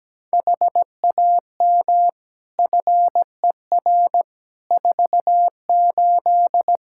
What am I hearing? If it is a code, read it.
Morse code, heard as a keyed tone, about 17 wpm, 700 Hz. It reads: HAM FER 48
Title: Practice QSO - Exchange 3 - More Info - 17wpm